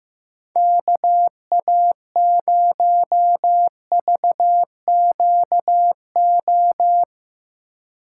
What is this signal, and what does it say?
Morse code, heard as a keyed tone, about 15 wpm, 700 Hz. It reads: KA0VQO
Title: USA Call Signs - 6 Characters - 2x Character Spacing - 15wpm